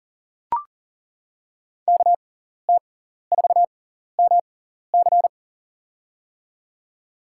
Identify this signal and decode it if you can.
Morse code, heard as a keyed tone, about 40 wpm, spaced as for 14 wpm, 700 Hz. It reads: KT4MC
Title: USA Call Signs - 5 Characters - 6x Character Spacing - 40wpm